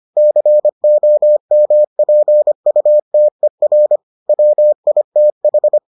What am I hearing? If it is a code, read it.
Morse code, heard as a keyed tone, about 25 wpm, 600 Hz. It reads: COMPUTER WITH